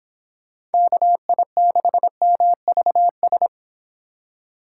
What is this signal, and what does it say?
Morse code, heard as a keyed tone, about 29 wpm, 700 Hz. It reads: KI6MVS